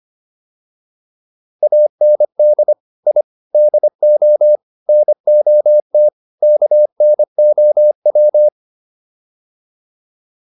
Morse code, heard as a keyed tone, about 25 wpm, 600 Hz. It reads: AND I DO NOT KNOW